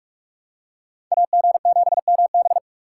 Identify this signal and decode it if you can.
Morse code, heard as a keyed tone, about 43 wpm, 700 Hz. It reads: AG7MB